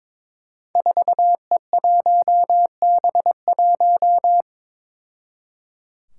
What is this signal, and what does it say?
Morse code, heard as a keyed tone, about 22 wpm, 700 Hz. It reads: 4E1B1
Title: Five Challenging Letter-Number Combinations 22wpm